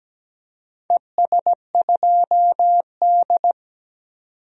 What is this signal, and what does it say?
Morse code, heard as a keyed tone, about 17 wpm, 700 Hz. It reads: ES2D
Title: DXCC-100 Call Signs - 4 Characters - 17wpm